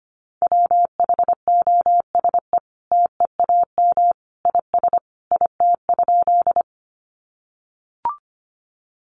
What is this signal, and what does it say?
Morse code, heard as a keyed tone, about 25 wpm, 700 Hz. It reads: WHOSE TEAM IS IT?
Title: Sentences from Top 1500 Words 25wpm